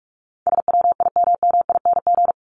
Morse code, heard as a keyed tone, about 44 wpm, 700 Hz. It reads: SWIMMING
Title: Top 1501-1750 Words 45wpm